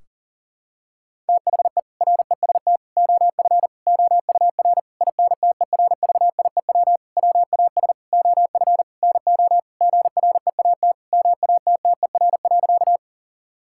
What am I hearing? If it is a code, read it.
Morse code, heard as a keyed tone, about 40 wpm, 700 Hz. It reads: THE REST OF OUR INTERVIEW WAS OF NO GREAT MATTER.
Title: Sentences from Top 1600 Words 40wpm